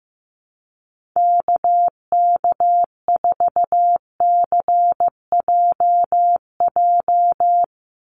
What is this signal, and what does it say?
Morse code, heard as a keyed tone, about 15 wpm, 700 Hz. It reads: KK4CJJ
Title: USA Call Signs 15wpm